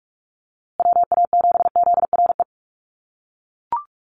Morse code, heard as a keyed tone, about 44 wpm, 700 Hz. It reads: WA7ZRE
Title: USA Call Signs Repeated 2x 45wpm